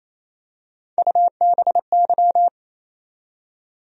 Morse code, heard as a keyed tone, about 28 wpm, 700 Hz. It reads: UBY
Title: USA Call Sign Postfixes - 3 Letters 28wpm